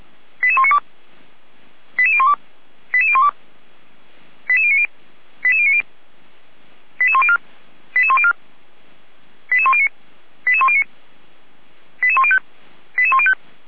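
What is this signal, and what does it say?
Five-tone selective call (ZVEI), heard as quick runs of five tones, tones of about 70 ms. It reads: 88192 88012 88012 88090 88090 88104 88104 88109 88109 88105 88105
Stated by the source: ZVEI call received by SDR, fire brigade Feuerwehr Orlamünde (f85360).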